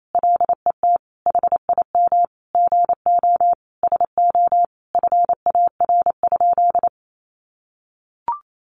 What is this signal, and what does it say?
Morse code, heard as a keyed tone, about 28 wpm, 700 Hz. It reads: LET HIM GO SO FAR?
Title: Sentences from Top 300 Words 28wpm